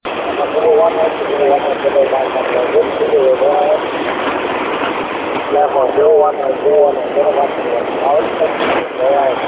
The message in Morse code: none

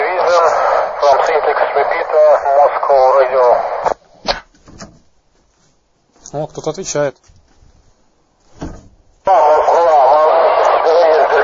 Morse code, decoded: none

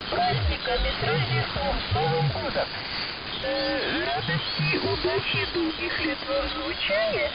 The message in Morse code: none

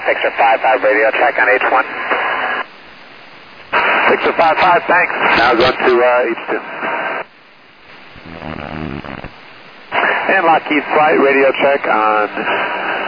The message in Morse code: none

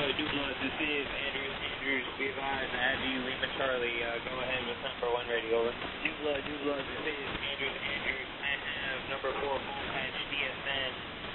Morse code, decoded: none

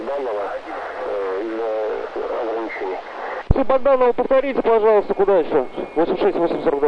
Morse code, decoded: none

